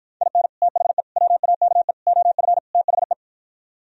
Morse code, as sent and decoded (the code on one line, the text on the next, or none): .. -. / - .... / .-- .- -.- . / --- ..-. / - .... .
IN THE WAKE OF THE